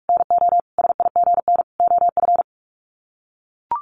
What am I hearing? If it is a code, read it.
Morse code, heard as a keyed tone, about 44 wpm, 700 Hz. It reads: NO SIGN OF
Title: Sets of 5 Words Encore2 45wpm